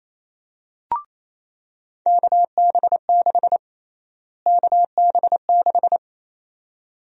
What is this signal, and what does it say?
Morse code, heard as a keyed tone, about 27 wpm, 700 Hz. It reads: KB6 KB6
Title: USA Call Sign Prefixes - Repeated 2x - 28wpm